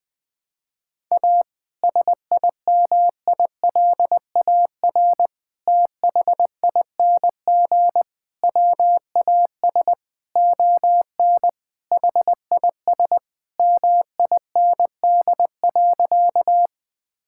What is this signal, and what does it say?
Morse code, heard as a keyed tone, about 20 wpm, 700 Hz. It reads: A SIMILAR THING WAS ON HIS MIND.
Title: Sentences from Top 1000 Words - Review Part 5 - 20wpm